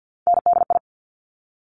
Morse code, heard as a keyed tone, about 50 wpm, 700 Hz. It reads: NDI
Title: Top Three Letter Combinations in Words 50wpm